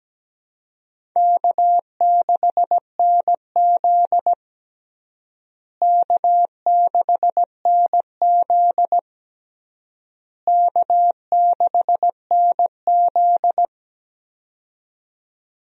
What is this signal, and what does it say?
Morse code, heard as a keyed tone, about 17 wpm, 700 Hz. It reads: K6NZ K6NZ K6NZ
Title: USA Call Signs - 4 Characters - Repeated 3x - 17wpm